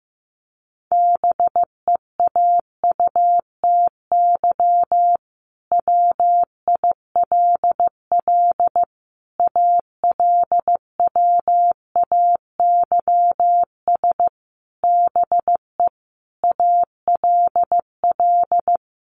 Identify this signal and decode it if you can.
Morse code, heard as a keyed tone, about 15 wpm, 700 Hz. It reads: BEAUTY WILL ALWAYS BE ALL
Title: Sentences from Top 1300 Words 15wpm